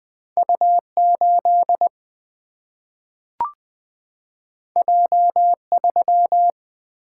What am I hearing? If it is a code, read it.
Morse code, heard as a keyed tone, about 20 wpm, 700 Hz. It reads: U 8 J 3
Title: Single-Letter Single-Number 20wpm